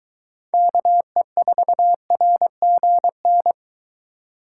Morse code, heard as a keyed tone, about 23 wpm, 700 Hz. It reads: KE4RGN